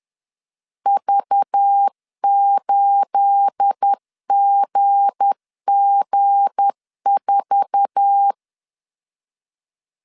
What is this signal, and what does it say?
Morse code, heard as a keyed tone, about 10 wpm, 800 Hz. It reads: V8GG4
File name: CW / DET / 1 9 1500C S A 10wpm 4511